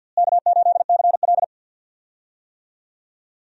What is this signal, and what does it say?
Morse code, heard as a keyed tone, about 50 wpm, 700 Hz. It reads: K8CL
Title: USA Call Signs - 4 Characters - Repeated 2x - 50wpm